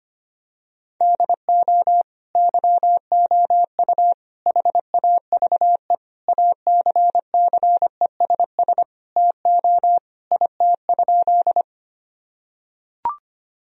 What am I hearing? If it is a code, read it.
Morse code, heard as a keyed tone, about 25 wpm, 700 Hz. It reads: DO YOU HAVE ACCESS TO IT?
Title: Sentences from Top 1000 Words - Review Part 4 - 25wpm